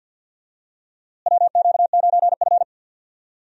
Morse code, heard as a keyed tone, about 50 wpm, 700 Hz. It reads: WQ9F
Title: USA Call Signs - 4 Characters - Repeated 2x - 50wpm